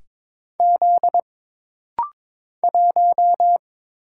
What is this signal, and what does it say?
Morse code, heard as a keyed tone, about 22 wpm, 700 Hz. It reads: Z 1
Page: This rapid-fire timing is geared to push your ICR (Instant Character Recognition) skills.